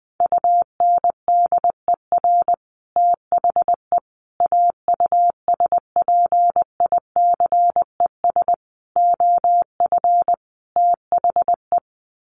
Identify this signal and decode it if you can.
Morse code, heard as a keyed tone, about 20 wpm, 700 Hz. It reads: UNDER THE AUSPICES OF THE
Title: Sets of 5 Words Encore2 20wpm